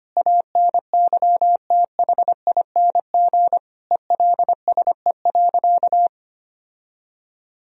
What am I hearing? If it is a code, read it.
Morse code, heard as a keyed tone, about 25 wpm, 700 Hz. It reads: ANYTHING ELSE.
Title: Sentences from Top 1000 Words - Review Part 1 - 25wpm